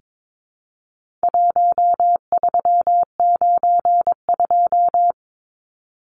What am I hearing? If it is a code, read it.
Morse code, heard as a keyed tone, about 22 wpm, 700 Hz. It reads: 1392
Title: Numbers - Contest Style - 4-Digit 22wpm